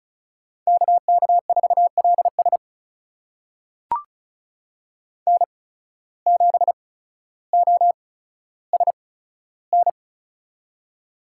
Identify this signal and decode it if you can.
Morse code, heard as a keyed tone, about 36 wpm, spaced as for 15 wpm, 700 Hz. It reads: KK4LS N7OSN
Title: USA Call Signs - 5 Characters - 8x Character Spacing - 35wpm